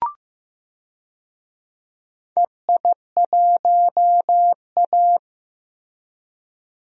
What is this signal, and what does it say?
Morse code, heard as a keyed tone, about 16 wpm, 700 Hz. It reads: EI1A
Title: DXCC-100 Call Signs - 4 Characters - Repeated 3x - 15wpm